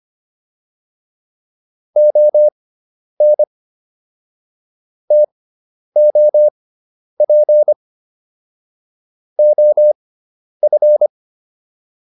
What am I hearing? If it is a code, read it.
Morse code, heard as a keyed tone, about 25 wpm, spaced as for 10 wpm, 600 Hz. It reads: ON TOP OF